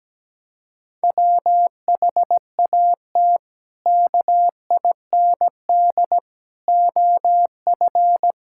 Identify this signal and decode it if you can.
Morse code, heard as a keyed tone, about 17 wpm, 700 Hz. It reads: WHAT KIND OF
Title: Sentences from Top 1000 Words - Review Part 5 - 17wpm